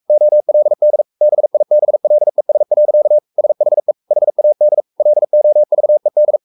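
Morse code, heard as a keyed tone, about 44 wpm, 600 Hz. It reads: OLD BIBLES. SHE HAD LOVED
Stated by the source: Morse code at 43 wpm sending Ken Follett's novel Never in English, from CW med Gnister.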